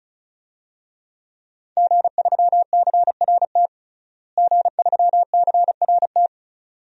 Morse code, heard as a keyed tone, about 34 wpm, 700 Hz. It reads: G3CRT G3CRT
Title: DXCC-100 Call Signs - 5 Characters - Repeated 2x - 35wpm